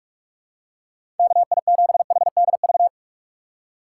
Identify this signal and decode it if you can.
Morse code, heard as a keyed tone, about 44 wpm, 700 Hz. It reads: KI7HDV
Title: USA Call Signs - 6x Character Spacing - 45wpm